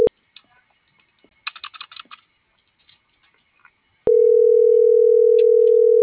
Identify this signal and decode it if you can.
Ringback tone (North America), 440 + 480 Hz.